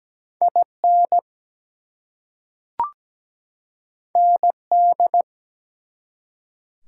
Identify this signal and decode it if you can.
Morse code, heard as a keyed tone, about 17 wpm, 700 Hz. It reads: IN ND
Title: US States and Canadian Provinces 17wpm